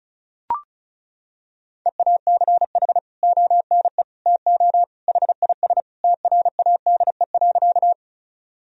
Morse code, heard as a keyed tone, about 36 wpm, 700 Hz. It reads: EACH ONE TO HIS TRADE.
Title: Sentences from Top 1000 Words - Review Part 4 - 35wpm